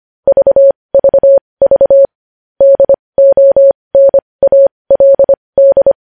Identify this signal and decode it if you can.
Morse code, heard as a keyed tone, about 25 wpm, spaced as for 20 wpm, 550 Hz. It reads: VVV DONALD
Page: News headlines in morse code at 20 wpm